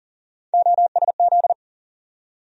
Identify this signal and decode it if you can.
Morse code, heard as a keyed tone, about 40 wpm, 700 Hz. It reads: OSZ
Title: USA Call Sign Postfixes - 3 Letters 40wpm